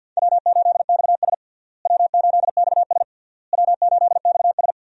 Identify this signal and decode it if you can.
Morse code, heard as a keyed tone, about 49 wpm, 700 Hz. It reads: W8XS W8XS W8XS
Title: USA Call Signs - 4 Characters - Repeated 3x - 50wpm